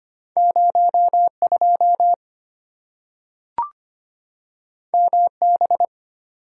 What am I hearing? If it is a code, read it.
Morse code, heard as a keyed tone, about 25 wpm, 700 Hz. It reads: 0 2 M B
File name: Two Letter-Number Combinations - 25wpm